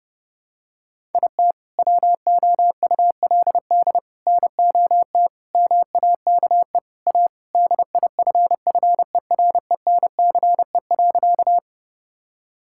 Morse code, heard as a keyed tone, about 32 wpm, 700 Hz. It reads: IT WOULD NOT MAKE A DIFFERENCE.
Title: Sentences from Top 1000 Words - Review Part 5 - 30wpm